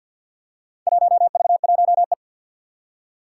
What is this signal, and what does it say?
Morse code, heard as a keyed tone, about 47 wpm, 700 Hz. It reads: 1V1E
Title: Four Challenging Letter-Number Combinations 50wpm